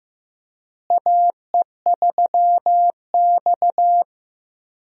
Morse code, heard as a keyed tone, about 15 wpm, 700 Hz. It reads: AE3X
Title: USA Call Signs - 4 Characters - Repeated 2x - 15wpm